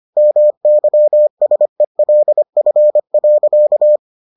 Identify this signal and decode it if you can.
Morse code, heard as a keyed tone, about 25 wpm, 600 Hz. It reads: MYSELF.